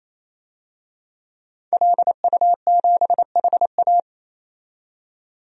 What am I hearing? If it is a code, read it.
Morse code, heard as a keyed tone, about 28 wpm, 700 Hz. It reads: LU7HA
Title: DXCC-100 Call Signs - 5 Characters - 28wpm